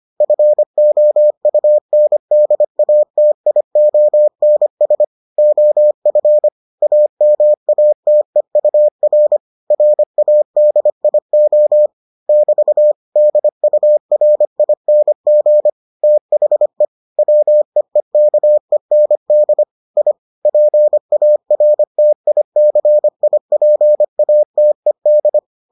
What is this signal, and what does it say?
Morse code, heard as a keyed tone, about 25 wpm, 600 Hz. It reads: FOUNDATIONS OF AMATEUR RADIO <BT> DURING THE WEEKEND I PARTICIPATED